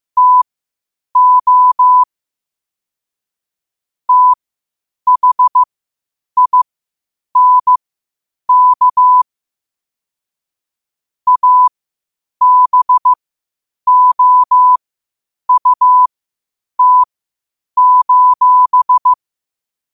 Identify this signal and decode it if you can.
Morse code, heard as a keyed tone, about 15 wpm, spaced as for 8 wpm, 1000 Hz. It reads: TO THINK ABOUT:
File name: qotdfiles / 8wpm